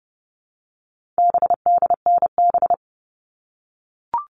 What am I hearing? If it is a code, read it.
Morse code, heard as a keyed tone, about 30 wpm, 700 Hz. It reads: BDNB
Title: All Cut Numbers - 4-Digit 30wpm